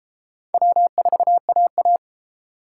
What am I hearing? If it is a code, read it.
Morse code, heard as a keyed tone, about 30 wpm, 700 Hz. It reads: W4AA